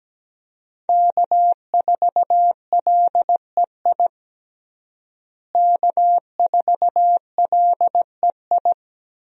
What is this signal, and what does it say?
Morse code, heard as a keyed tone, about 17 wpm, 700 Hz. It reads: K4LEI K4LEI
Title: USA Call Signs - 5 Characters - Repeated 3x - 17wpm